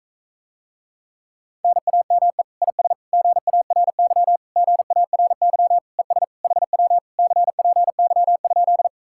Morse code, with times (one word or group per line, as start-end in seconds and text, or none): 1.6-2.4 NAME
2.6-2.9 IS
3.1-4.4 GARY
4.6-5.8 GARY
6.0-6.2 ES
6.4-7.0 HW
7.2-8.9 CPY?